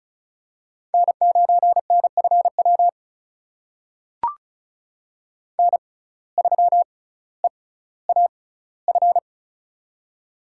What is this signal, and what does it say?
Morse code, heard as a keyed tone, about 36 wpm, spaced as for 18 wpm, 700 Hz. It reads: N9NFW N3EAF
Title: USA Call Signs - 5 Characters - 6x Character Spacing - 35wpm